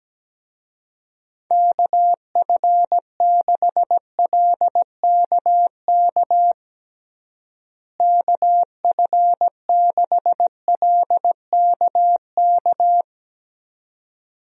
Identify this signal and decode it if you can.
Morse code, heard as a keyed tone, about 17 wpm, 700 Hz. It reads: KF6LKK KF6LKK
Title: USA Call Signs - 6 Characters - Repeated 2x - 17wpm